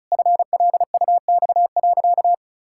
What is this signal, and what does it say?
Morse code, heard as a keyed tone, about 36 wpm, 700 Hz. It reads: FLUX.
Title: Pangram Challenge v2 - 35wpm